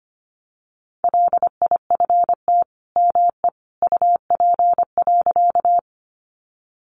Morse code, heard as a keyed tone, about 25 wpm, 700 Hz. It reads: LIFT ME UP.
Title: Sentences from Top 1000 Words 25wpm